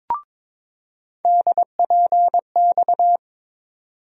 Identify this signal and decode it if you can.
Morse code, heard as a keyed tone, about 22 wpm, 700 Hz. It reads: DPX
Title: USA Call Sign Postfixes - 3 Letters 22wpm